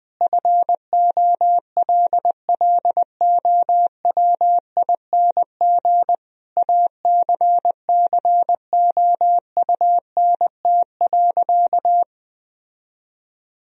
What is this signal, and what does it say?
Morse code, heard as a keyed tone, about 21 wpm, 700 Hz. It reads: FOLLOWING ACCOUNT.
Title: Sentences from Top 1000 Words - Review Part 4 - 20wpm